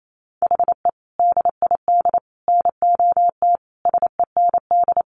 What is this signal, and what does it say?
Morse code, heard as a keyed tone, about 28 wpm, 700 Hz. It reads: HE DID NOT SEND
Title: Sentences from Top 1500 Words 28wpm